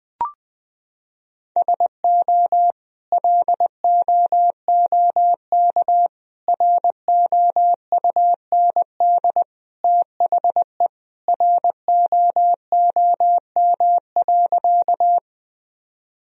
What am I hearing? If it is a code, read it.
Morse code, heard as a keyed tone, about 20 wpm, 700 Hz. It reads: SO LOOK ROUND THE ROOM.
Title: Sentences from Top 1000 Words - Review Part 1 - 20wpm